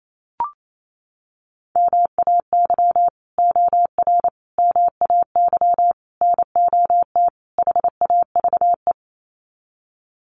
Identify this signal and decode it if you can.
Morse code, heard as a keyed tone, about 28 wpm, 700 Hz. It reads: MAY OR MAY NOT HAVE
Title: Sets of 5 Words Encore2 28wpm